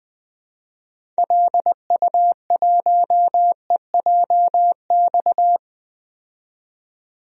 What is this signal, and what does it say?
Morse code, heard as a keyed tone, about 20 wpm, 700 Hz. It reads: LU1EJX
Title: DXCC-100 Call Signs - 6 Characters - 20wpm